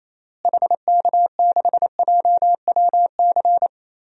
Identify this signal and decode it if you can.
Morse code, heard as a keyed tone, about 28 wpm, 700 Hz. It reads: HK6JWC
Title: DXCC-100 Call Signs - 6 Characters - Repeated 3x - 28wpm